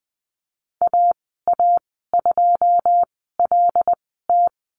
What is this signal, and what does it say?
Morse code, heard as a keyed tone, about 20 wpm, spaced as for 15 wpm, 700 Hz. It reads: AA2LT
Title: 082 Lesson 14g - Number 2 - Call Signs 20wpm